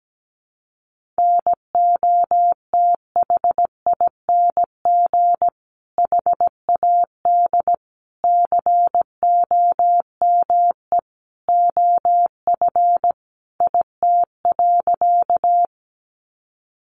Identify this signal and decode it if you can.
Morse code, heard as a keyed tone, about 17 wpm, 700 Hz. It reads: NOTHING HAD COME OF IT.
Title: Sentences from Top 500 Words 17wpm